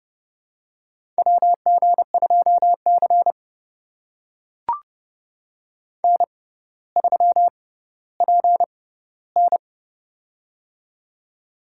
Morse code, heard as a keyed tone, about 30 wpm, spaced as for 15 wpm, 700 Hz. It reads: WG2C N3PN
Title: USA Call Signs - 4 Characters - 6x Character Spacing - 30wpm